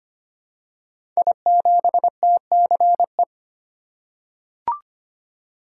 Morse code, heard as a keyed tone, about 25 wpm, 700 Hz. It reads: I7TCE
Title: Code Groups - 5 Letter:Numbers - 25wpm